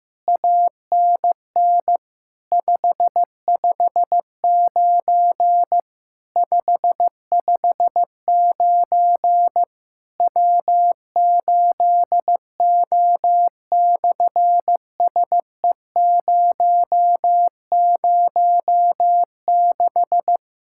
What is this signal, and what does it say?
Morse code, heard as a keyed tone, about 15 wpm, 700 Hz. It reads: ANN 559 559 W8O/SE006
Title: SOTA QSO Practice 15wpm